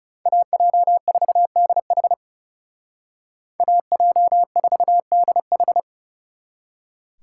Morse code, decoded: AJ4DH AJ4DH

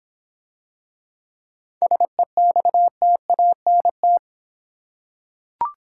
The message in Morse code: SEXTANT

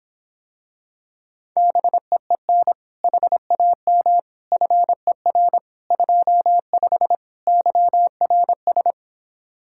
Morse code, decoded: BEEN HAM FER 25 YRS

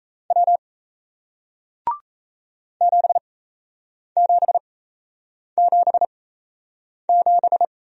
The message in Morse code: W 7777

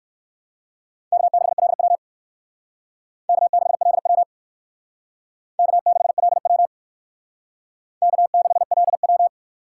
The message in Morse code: K6LW K6LW K6LW K6LW

